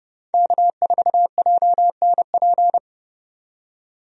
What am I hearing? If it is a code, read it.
Morse code, heard as a keyed tone, about 30 wpm, 700 Hz. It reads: K4JNP
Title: USA Call Signs - 5 Characters - 30wpm